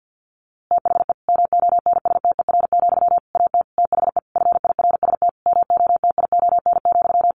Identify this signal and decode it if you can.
Morse code, heard as a keyed tone, about 49 wpm, 700 Hz. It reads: THE MONSTER, AT THE FIRST MOTION,
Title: A Sound Of Thunder 50wpm